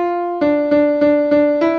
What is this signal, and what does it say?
There is no code.